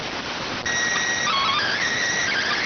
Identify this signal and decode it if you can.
SSTV (slow-scan television), PD 290.